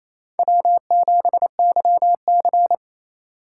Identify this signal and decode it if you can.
Morse code, heard as a keyed tone, about 28 wpm, 700 Hz. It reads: W7YC
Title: USA Call Signs - 4 Characters - 2x Character Spacing - 28wpm